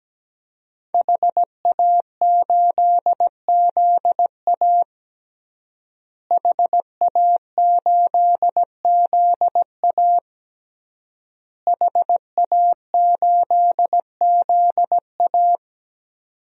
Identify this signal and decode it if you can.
Morse code, heard as a keyed tone, about 17 wpm, 700 Hz. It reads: HA8ZA HA8ZA HA8ZA
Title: DXCC-100 Call Signs - 5 Characters - Repeated 3x - 17wpm